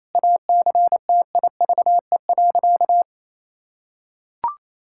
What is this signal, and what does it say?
Morse code, heard as a keyed tone, about 28 wpm, 700 Hz. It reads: ACTIVE.
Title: Sentences from Top 1000 Words - Review Part 6 - 28wpm